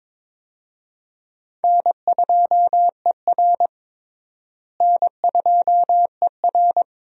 Morse code, heard as a keyed tone, about 22 wpm, 700 Hz. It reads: N2ER N2ER
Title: USA Call Signs - 4 Characters - Repeated 3x - 22wpm